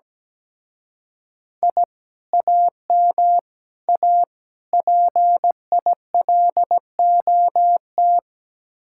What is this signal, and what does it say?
Morse code, heard as a keyed tone, about 17 wpm, 700 Hz. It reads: I AM A PILOT
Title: Speed-Racing Common QSO Phrases Encore - 17wpm